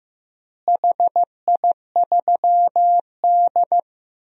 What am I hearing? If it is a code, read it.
Morse code, heard as a keyed tone, about 15 wpm, 700 Hz. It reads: HI3D
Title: DXCC-100 Call Signs - 4 Characters - Repeated 2x - 15wpm